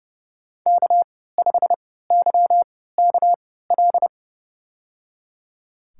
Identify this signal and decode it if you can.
Morse code, heard as a keyed tone, about 30 wpm, spaced as for 17 wpm, 700 Hz. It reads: K5YKL
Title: USA Call Signs - 5 Characters - 3x Character Spacing - 30wpm